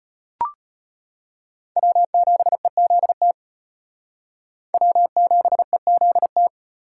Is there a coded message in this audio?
Yes: Morse code from 1.8 to 6.5 s, heard as a keyed tone, about 38 wpm, 700 Hz. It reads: W7EZT W7EZT